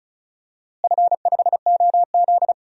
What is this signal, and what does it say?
Morse code, heard as a keyed tone, about 34 wpm, 700 Hz. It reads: F5OZ
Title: DXCC-100 Call Signs - 4 Characters - Repeated 2x - 35wpm